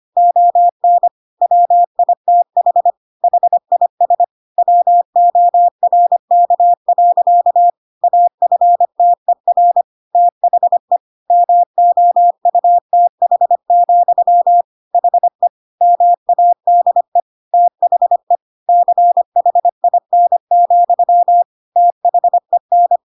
Morse code, heard as a keyed tone, about 25 wpm, 700 Hz. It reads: ON WITH HIS WORK. AFTER THE MOUTH, HE MADE THE CHIN, THEN
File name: Pinocchio / Pinocchio ch3 25 25 ik1hge